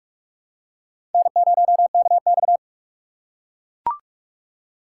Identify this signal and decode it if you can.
Morse code, heard as a keyed tone, about 40 wpm, 700 Hz. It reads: N0KX